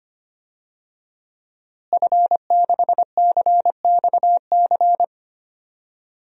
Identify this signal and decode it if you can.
Morse code, heard as a keyed tone, about 25 wpm, 700 Hz. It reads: F6CXC
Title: DXCC-100 Call Signs - 5 Characters - Repeated 2x - 25wpm